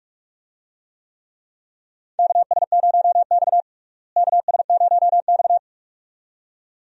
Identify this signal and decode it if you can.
Morse code, heard as a keyed tone, about 46 wpm, 700 Hz. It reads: KS0X KS0X
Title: USA Call Signs - 4 Characters - Repeated 2x - 45wpm